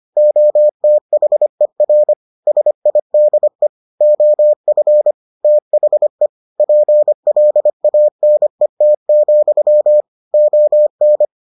Morse code, heard as a keyed tone, about 25 wpm, 600 Hz. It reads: OTHER SIDE OF THE PLANET, ON